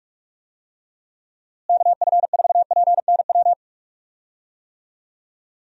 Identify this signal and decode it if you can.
Morse code, heard as a keyed tone, about 46 wpm, 700 Hz. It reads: KF4PNW
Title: USA Call Signs - 6 Characters - Repeated 2x - 45wpm